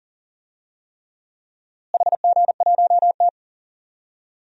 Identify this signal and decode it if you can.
Morse code, heard as a keyed tone, about 40 wpm, 700 Hz. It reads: HG1T
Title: DXCC-100 Call Signs - 4 Characters - Repeated 3x - 40wpm